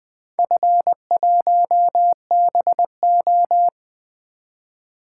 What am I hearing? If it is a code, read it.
Morse code, heard as a keyed tone, about 20 wpm, 700 Hz. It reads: F1BO
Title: Four Letter-Number Combinations - 20wpm